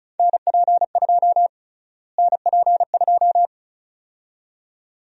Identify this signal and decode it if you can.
Morse code, heard as a keyed tone, about 36 wpm, 700 Hz. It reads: NP2 NP2
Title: USA Call Sign Prefixes - Repeated 2x - 35wpm